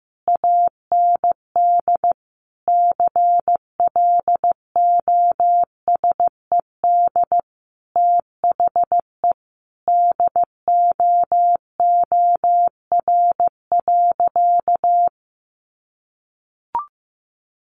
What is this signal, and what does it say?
Morse code, heard as a keyed tone, about 15 wpm, 700 Hz. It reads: AND CLOSED THE DOOR.
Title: Sentences from Top 1400 Words 15wpm